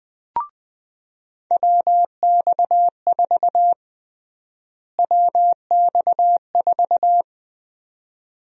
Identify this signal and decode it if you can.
Morse code, heard as a keyed tone, about 21 wpm, 700 Hz. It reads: WX4 WX4
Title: USA Call Sign Prefixes - Repeated 2x - 20wpm